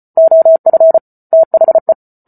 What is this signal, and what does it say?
Morse code, heard as a keyed tone, about 34 wpm, 650 Hz. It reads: OF THE